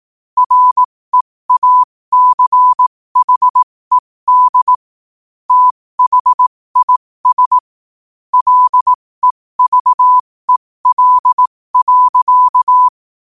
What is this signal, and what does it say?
Morse code, heard as a keyed tone, about 18 wpm, spaced as for 15 wpm, 1000 Hz. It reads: REACHED THIS LEVEL.
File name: qotdfiles / 15wpm